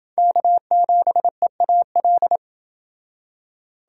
Morse code, heard as a keyed tone, about 30 wpm, 700 Hz. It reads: K7EAL